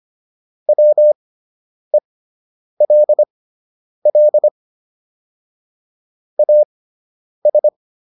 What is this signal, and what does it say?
Morse code, heard as a keyed tone, about 25 wpm, spaced as for 9 wpm, 600 Hz. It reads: WELL AS